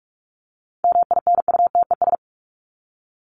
Morse code, heard as a keyed tone, about 44 wpm, 700 Hz. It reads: MINUTES
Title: ES:S Suffix - Top 100 Words with Base Words 45wpm